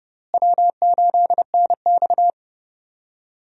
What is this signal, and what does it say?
Morse code, heard as a keyed tone, about 31 wpm, 700 Hz. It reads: W8NX
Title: USA Call Signs - 4 Characters - Repeated 3x - 30wpm